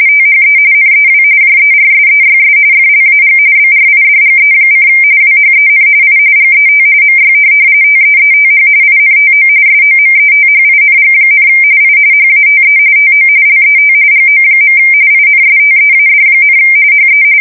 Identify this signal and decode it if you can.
RTTY, about 45.45 baud. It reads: THE NEW JERSEYS FLAG BRIDGE: WHEN I GET MY TEETH INTO SOMETHING, I HATE TO LET GO. INSTEAD, HE
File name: fsk170 / USS-New-Jersey